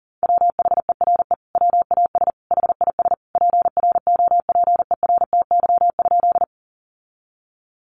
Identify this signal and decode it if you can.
Morse code, heard as a keyed tone, about 40 wpm, 700 Hz. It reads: WHERE WAS HIS PROPERTY?
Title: Sentences from Top 1000 Words 40wpm